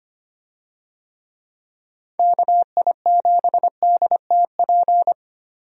Morse code, heard as a keyed tone, about 25 wpm, 700 Hz. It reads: KI7DTP
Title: USA Call Signs - 6 Characters - Repeated 3x - 25wpm